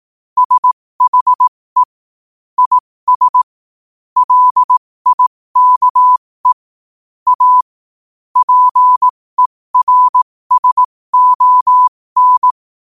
Morse code, heard as a keyed tone, about 18 wpm, spaced as for 15 wpm, 1000 Hz. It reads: SHE IS LIKE A PERSON
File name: qotdfiles / 15wpm